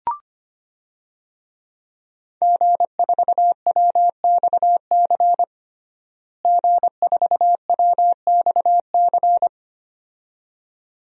Morse code, heard as a keyed tone, about 25 wpm, 700 Hz. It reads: G4WXC G4WXC
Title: DXCC-100 Call Signs - 5 Characters - Repeated 2x - 25wpm